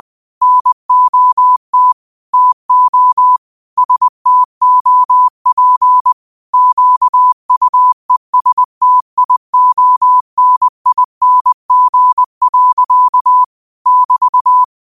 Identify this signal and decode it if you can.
Morse code, heard as a keyed tone, about 20 wpm, 1000 Hz. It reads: NOT TO STOP QUESTIONING. <BT>
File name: qotdfiles / 20wpm